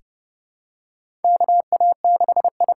Morse code, heard as a keyed tone, about 30 wpm, 700 Hz. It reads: KA6I
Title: USA Call Signs - 6 Characters - 8x Character Spacing - 30wpm